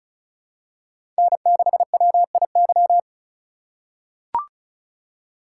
Morse code, heard as a keyed tone, about 36 wpm, 700 Hz. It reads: N6WIY
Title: USA Call Signs - 4x Character Spacing - 35wpm